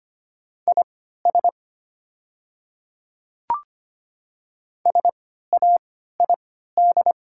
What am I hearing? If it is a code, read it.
Morse code, heard as a keyed tone, about 25 wpm, spaced as for 14 wpm, 700 Hz. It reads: IS SAID
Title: 3x Character Spacing - Top 100 Words - 25wpm